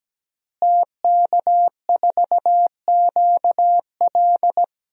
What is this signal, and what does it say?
Morse code, heard as a keyed tone, about 17 wpm, 700 Hz. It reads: TK4QL
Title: DXCC-100 Call Signs - 5 Characters - Repeated 3x - 17wpm